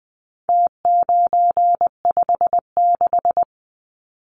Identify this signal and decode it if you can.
Morse code, heard as a keyed tone, about 20 wpm, 700 Hz. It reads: T956
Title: Numbers - Contest Style - 4-Digit 20wpm